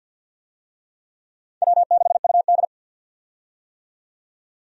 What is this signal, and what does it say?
Morse code, heard as a keyed tone, about 53 wpm, 700 Hz. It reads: W6UD
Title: USA Call Signs - 4 Characters - 50wpm